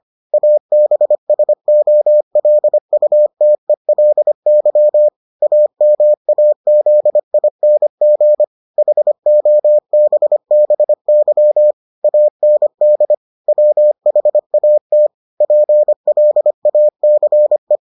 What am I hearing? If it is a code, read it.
Morse code, heard as a keyed tone, about 25 wpm, 600 Hz. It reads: ABSOLUTELY AMAZING HOBBY AND WHAT PLACE